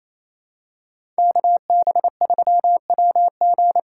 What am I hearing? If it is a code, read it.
Morse code, heard as a keyed tone, about 27 wpm, 700 Hz. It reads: KB3WG
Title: USA Call Signs - 6 Characters - 8x Character Spacing - 28wpm